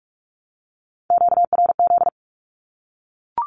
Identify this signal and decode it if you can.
Morse code, heard as a keyed tone, about 44 wpm, 700 Hz. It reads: QRZ
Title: Q-Codes 45wpm